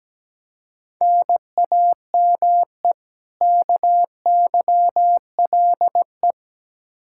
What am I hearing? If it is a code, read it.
Morse code, heard as a keyed tone, about 17 wpm, 700 Hz. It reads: NAME KYLE
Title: Practice QSO - Exchange 1 - Beginning - 17wpm